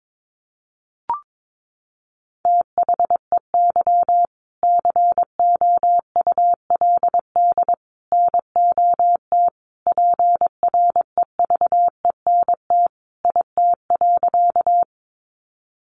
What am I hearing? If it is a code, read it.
Morse code, heard as a keyed tone, about 22 wpm, 700 Hz. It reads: THEY COULD NOT PREVENT IT.
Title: Sentences from Top 1100 Words 22wpm